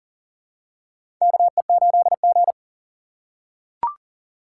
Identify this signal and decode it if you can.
Morse code, heard as a keyed tone, about 40 wpm, 700 Hz. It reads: KE8G
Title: USA Call Signs - 4 Characters - 2x Character Spacing - 40wpm